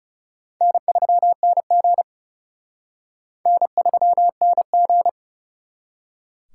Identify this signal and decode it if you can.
Morse code, heard as a keyed tone, about 38 wpm, 700 Hz. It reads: N3NG N3NG